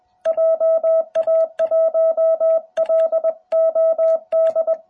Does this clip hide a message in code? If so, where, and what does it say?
Yes: Morse code from 0.3 to 4.7 s, heard as a keyed tone, about 20 wpm, 650 Hz. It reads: JA1LOD